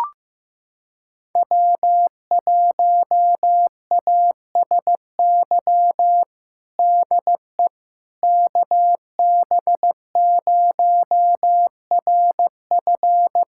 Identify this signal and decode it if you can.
Morse code, heard as a keyed tone, about 16 wpm, 700 Hz. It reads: W1ASY DE KB0RF
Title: USA Callsign DE Callsign - 5 Characters - 15wpm